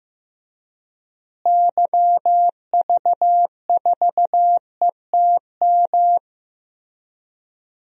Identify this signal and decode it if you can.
Morse code, heard as a keyed tone, about 15 wpm, 700 Hz. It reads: YV4ETM
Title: DXCC-100 Call Signs - 6 Characters - Repeated 3x - 15wpm